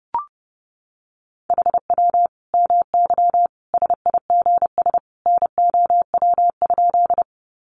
Morse code, heard as a keyed tone, about 30 wpm, 700 Hz. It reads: HW MY SIGS NOW?
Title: Common QSO Phrases 30wpm